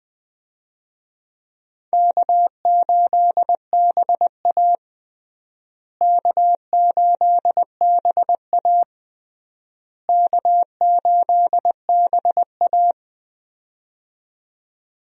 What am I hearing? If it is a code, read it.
Morse code, heard as a keyed tone, about 21 wpm, 700 Hz. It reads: K8BA K8BA K8BA
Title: USA Call Signs - 4 Characters - Repeated 3x - 20wpm